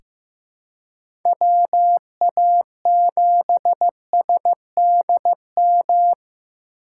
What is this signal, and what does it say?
Morse code, heard as a keyed tone, about 15 wpm, 700 Hz. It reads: WA7SDM
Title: USA Call Signs - 6 Characters - 6x Character Spacing - 15wpm